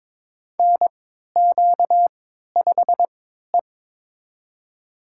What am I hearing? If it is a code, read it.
Morse code, heard as a keyed tone, about 22 wpm, spaced as for 13 wpm, 700 Hz. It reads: NQ5E
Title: USA Call Signs - 3x Character Spacing - 22wpm